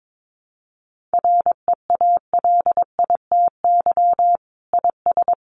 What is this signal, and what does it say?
Morse code, heard as a keyed tone, about 22 wpm, 700 Hz. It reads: REALITY IS
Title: Sentences from Top 1200 Words 22wpm